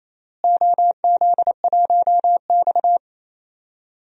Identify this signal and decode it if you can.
Morse code, heard as a keyed tone, about 28 wpm, 700 Hz. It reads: OZ1X
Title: DXCC-100 Call Signs - 4 Characters - Repeated 3x - 28wpm